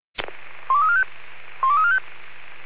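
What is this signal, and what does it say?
Five-tone selective call (ZVEI), heard as quick runs of five tones, tones of about 70 ms. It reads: 12345 12345
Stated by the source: ZVEI five-tone call received on SDR.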